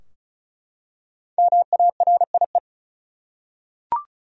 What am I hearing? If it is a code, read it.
Morse code, heard as a keyed tone, about 36 wpm, 700 Hz. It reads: MARIE
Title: USA Common Names - Max 5 Letters - 35wpm